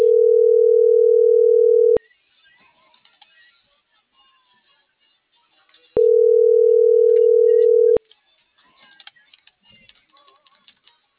Ringback tone (North America), 440 + 480 Hz.